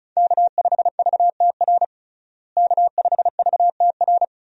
Morse code, heard as a keyed tone, about 34 wpm, 700 Hz. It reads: K5VTR K5VTR
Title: USA Call Signs - 5 Characters - Repeated 2x - 35wpm